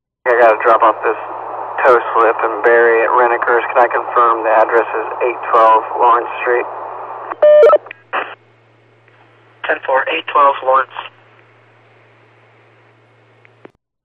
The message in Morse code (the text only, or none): none